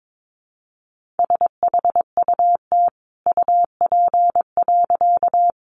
Morse code, heard as a keyed tone, about 22 wpm, 700 Hz. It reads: SHUT UP.
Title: A Sound Of Thunder 22wpm